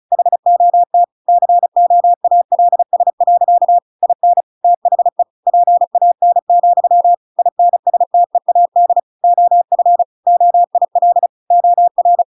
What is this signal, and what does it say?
Morse code, heard as a keyed tone, about 35 wpm, 700 Hz. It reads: HOT COALS. IN THE PAN, INSTEAD OF OIL OR